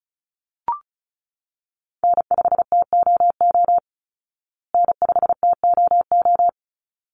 Morse code, heard as a keyed tone, about 34 wpm, 700 Hz. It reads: N5TOO N5TOO
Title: Call Signs Repeated 3x - 35wpm